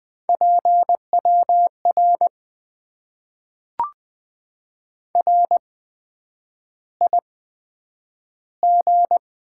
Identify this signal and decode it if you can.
Morse code, heard as a keyed tone, about 20 wpm, spaced as for 9 wpm, 700 Hz. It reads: PWR RIG
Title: Top 90 QSO Elements - 8x Character Spacing - 20wpm